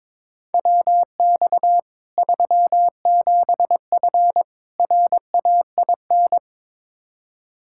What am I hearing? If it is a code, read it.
Morse code, heard as a keyed tone, about 22 wpm, 700 Hz. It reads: WX 37F RAIN
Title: Practice QSO - Exchange 2 - Info - 22wpm